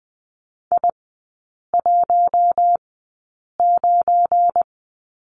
Morse code, heard as a keyed tone, about 20 wpm, spaced as for 8 wpm, 700 Hz. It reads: I19
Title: 184 Lesson 30e - Letter b - Three Characters 20wpm